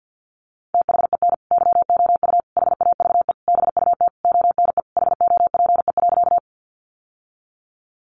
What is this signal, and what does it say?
Morse code, heard as a keyed tone, about 48 wpm, 700 Hz. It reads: THEN YOU HAVE BUT ONE HOPE.
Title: Sentences from Top 1000 Words 50wpm